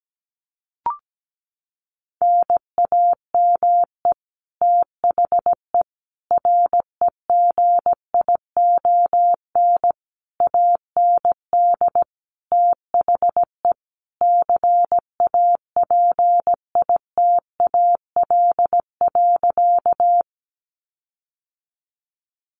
Morse code, heard as a keyed tone, about 17 wpm, 700 Hz. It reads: NAME THE REGION AND THE CAPITAL.
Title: Sentences from Top 1500 Words 17wpm